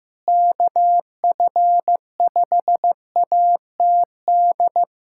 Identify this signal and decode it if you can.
Morse code, heard as a keyed tone, about 15 wpm, 700 Hz. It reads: KF5ATD
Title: USA Callsign DE Callsign - 6 Characters - 15wpm